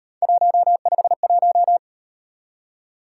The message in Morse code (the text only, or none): 151